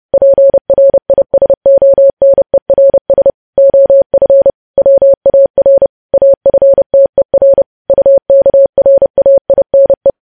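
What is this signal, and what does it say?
Morse code, heard as a keyed tone, about 30 wpm, 550 Hz. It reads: PRISONERS OF WAR AFTER UKRAINE
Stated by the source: Morse news headlines sent at 30.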